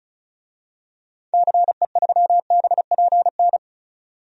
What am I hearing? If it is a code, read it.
Morse code, heard as a keyed tone, about 34 wpm, 700 Hz. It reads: CE3BPN
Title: DXCC-100 Call Signs - 6 Characters - 35wpm